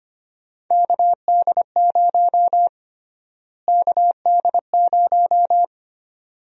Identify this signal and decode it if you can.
Morse code, heard as a keyed tone, about 25 wpm, 700 Hz. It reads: KD0 KD0
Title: USA Call Sign Prefixes - Repeated 2x - 25wpm